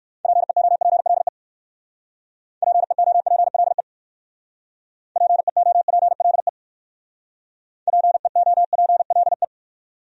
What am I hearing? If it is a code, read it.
Morse code, heard as a keyed tone, about 57 wpm, 700 Hz. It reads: PEOPLE PEOPLE PEOPLE PEOPLE